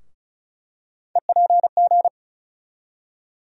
Morse code, heard as a keyed tone, about 36 wpm, 700 Hz. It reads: EPG
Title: USA Call Sign Postfixes - 3 Letters 35wpm